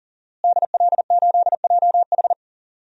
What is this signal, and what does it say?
Morse code, heard as a keyed tone, about 40 wpm, 700 Hz. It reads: DL8JH
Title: DXCC-100 Call Signs - 5 Characters - Repeated 3x - 40wpm